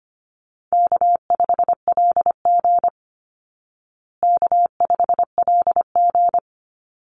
Morse code, heard as a keyed tone, about 25 wpm, 700 Hz. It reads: K5LG K5LG
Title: Call Signs Repeated 3x - 25wpm